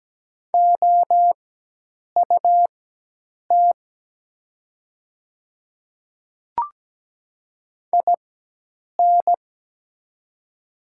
Morse code, heard as a keyed tone, about 17 wpm, spaced as for 8 wpm, 700 Hz. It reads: OUT IN
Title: 4x Character Spacing - Top 300 Words - 17wpm